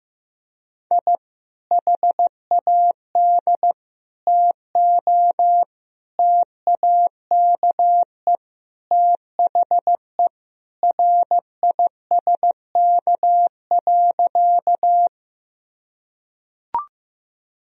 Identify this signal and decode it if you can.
Morse code, heard as a keyed tone, about 16 wpm, 700 Hz. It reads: I HAD TO TAKE THE RISK.
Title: Sentences from Top 1000 Words - Review Part 4 - 15wpm